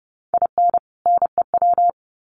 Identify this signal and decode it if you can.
Morse code, heard as a keyed tone, about 30 wpm, 700 Hz. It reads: IN NEW
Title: Sets of 2 Words 30wpm